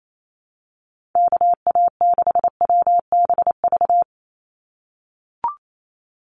Morse code, heard as a keyed tone, about 28 wpm, 700 Hz. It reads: KA6WBV
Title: USA Call Signs 28wpm